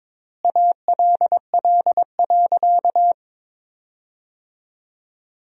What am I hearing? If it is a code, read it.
Morse code, heard as a keyed tone, about 22 wpm, 700 Hz. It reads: ALL.
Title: 4x Word-Spacing - Sentences from Top 200 Words - 22wpm